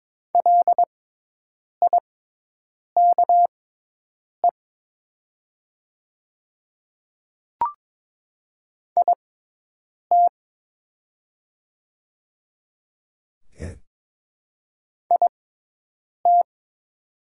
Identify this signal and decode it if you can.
Morse code, heard as a keyed tone, about 22 wpm, spaced as for 8 wpm, 700 Hz. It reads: LIKE IT IT
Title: 6x Character Spacing - Top 100 Words - 22wpm